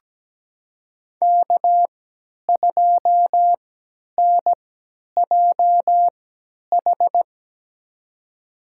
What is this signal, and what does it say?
Morse code, heard as a keyed tone, about 17 wpm, spaced as for 10 wpm, 700 Hz. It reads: K2NJH
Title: USA Call Signs - 5 Characters - 3x Character Spacing - 17wpm